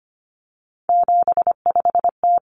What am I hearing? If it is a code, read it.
Morse code, heard as a keyed tone, about 25 wpm, 700 Hz. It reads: 75T
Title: Common Cut Numbers - 3-Digit 25wpm